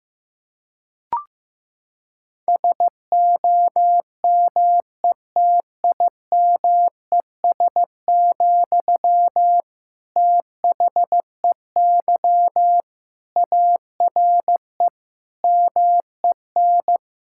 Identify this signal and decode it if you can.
Morse code, heard as a keyed tone, about 15 wpm, 700 Hz. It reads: SOMETIMES, THEY ARE MEN
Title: Sentences from Top 1000 Words - Review Part 5 - 15wpm